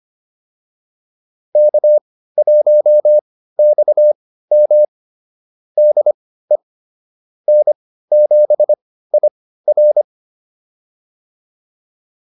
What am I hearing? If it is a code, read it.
Morse code, heard as a keyed tone, about 25 wpm, spaced as for 15 wpm, 600 Hz. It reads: K1XM DE N7IR